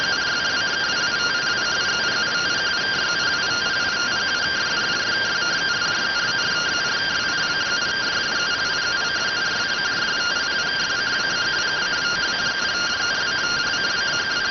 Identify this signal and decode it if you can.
Two-tone teleprinter signal, about 170 Hz shift.